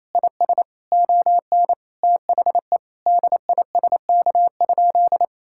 Morse code, heard as a keyed tone, about 28 wpm, 700 Hz. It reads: IS ON THE DISK?
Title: Sentences from Top 1000 Words - Review Part 6 - 28wpm